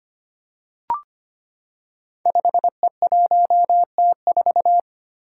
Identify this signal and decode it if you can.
Morse code, heard as a keyed tone, about 25 wpm, 700 Hz. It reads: 5E1T4
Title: Five Challenging Letter-Number Combinations 25wpm